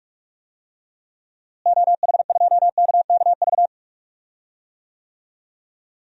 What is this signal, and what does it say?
Morse code, heard as a keyed tone, about 41 wpm, 700 Hz. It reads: OH2KKV
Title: DXCC-100 Call Signs - 6 Characters - Repeated 3x - 45wpm